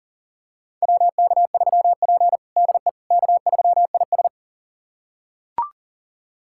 Morse code, heard as a keyed tone, about 40 wpm, 700 Hz. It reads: WK3P DE K3IS
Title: USA Callsign DE Callsign - 4 Characters - 40wpm